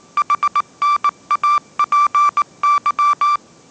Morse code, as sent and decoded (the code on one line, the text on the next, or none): .... -. .- .--. -.--
HNAPY